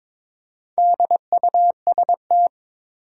Morse code, heard as a keyed tone, about 22 wpm, 700 Hz. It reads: DUST
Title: Four Letter Words - 22wpm